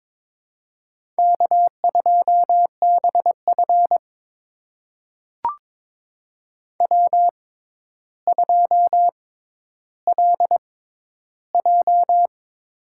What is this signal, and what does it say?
Morse code, heard as a keyed tone, about 22 wpm, spaced as for 11 wpm, 700 Hz. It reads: K2BF W2LJ
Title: USA Call Signs - 4 Characters - 6x Character Spacing - 22wpm